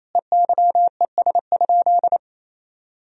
Morse code, heard as a keyed tone, about 28 wpm, 700 Hz. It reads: EYES?
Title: Sentences from Top 1000 Words - Review Part 4 - 28wpm